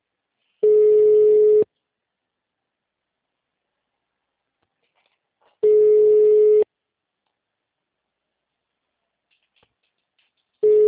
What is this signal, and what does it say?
Ringback tone (Europe, 425 Hz).
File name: out- -2037- - - .71